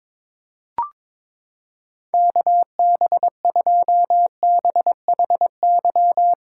Morse code, heard as a keyed tone, about 22 wpm, 700 Hz. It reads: KB2BHY
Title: USA Callsign DE Callsign - 6 Characters - 22wpm